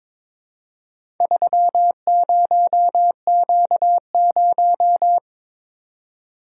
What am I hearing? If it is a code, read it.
Morse code, heard as a keyed tone, about 23 wpm, 700 Hz. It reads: 30Q0
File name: practice / 22wpm / Four Letter-Number Combinations - 22wpm